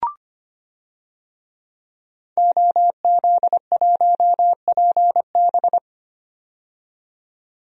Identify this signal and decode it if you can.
Morse code, heard as a keyed tone, about 25 wpm, 700 Hz. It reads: OZ1PB
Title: DXCC-100 Call Signs - 5 Characters - 25wpm